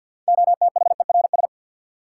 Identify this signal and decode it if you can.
Morse code, heard as a keyed tone, about 53 wpm, 700 Hz. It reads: OTHERS
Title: Sentences from Top 1000 Words - Review Part 2 - New Word Drill - 50wpm